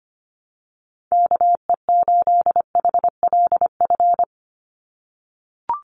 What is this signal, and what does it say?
Morse code, heard as a keyed tone, about 25 wpm, 700 Hz. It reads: KE8HLF
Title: USA Call Signs 25wpm